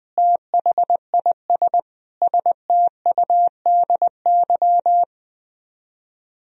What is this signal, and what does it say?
Morse code, heard as a keyed tone, about 21 wpm, 700 Hz. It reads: THIS STUDY
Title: Speed-Racing - Sets of 5 Words 20wpm